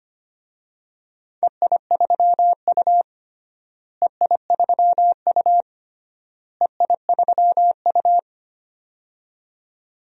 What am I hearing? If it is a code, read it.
Morse code, heard as a keyed tone, about 25 wpm, 700 Hz. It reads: EI3U EI3U EI3U
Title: DXCC-100 Call Signs - 4 Characters - Repeated 3x - 25wpm